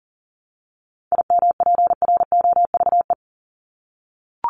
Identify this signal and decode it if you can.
Morse code, heard as a keyed tone, about 40 wpm, 700 Hz. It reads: IMPROVE
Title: Top 501-750 Words 40wpm